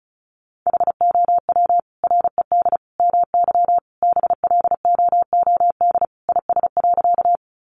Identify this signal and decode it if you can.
Morse code, heard as a keyed tone, about 34 wpm, 700 Hz. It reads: HOW RED MY BLOOD IS.
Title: Sentences from Top 1100 Words 35wpm